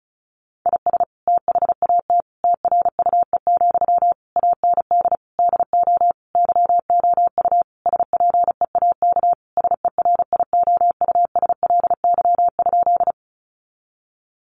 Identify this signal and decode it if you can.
Morse code, heard as a keyed tone, about 34 wpm, 700 Hz. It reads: IS THAT TRUE, AND DO YOU SPEAK SERIOUSLY?
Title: Sentences from Top 1200 Words 35wpm